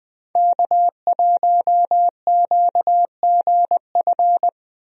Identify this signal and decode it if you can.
Morse code, heard as a keyed tone, about 20 wpm, 700 Hz. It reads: K1QGF
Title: USA Call Signs - 5 Characters - Repeated 3x - 20wpm